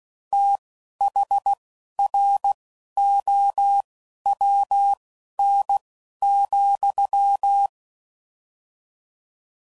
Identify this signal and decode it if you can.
Morse code, heard as a keyed tone, about 16 wpm, spaced as for 11 wpm, 800 Hz. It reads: THROWN,
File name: Prefixes / FLs & THs / Fast TH Prefixes